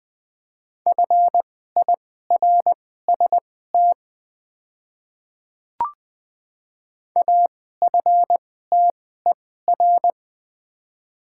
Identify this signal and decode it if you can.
Morse code, heard as a keyed tone, about 21 wpm, spaced as for 15 wpm, 700 Hz. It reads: FIRST AFTER
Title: 2x Character Spacing - Top 300 Words - 20wpm